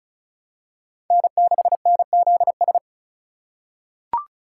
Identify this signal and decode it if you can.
Morse code, heard as a keyed tone, about 36 wpm, 700 Hz. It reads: N6NZS
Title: USA Call Signs - 4x Character Spacing - 35wpm